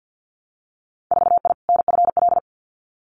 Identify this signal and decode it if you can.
Morse code, heard as a keyed tone, about 50 wpm, 700 Hz. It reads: 4I NFL
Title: Winter Field Day Exchanges 50wpm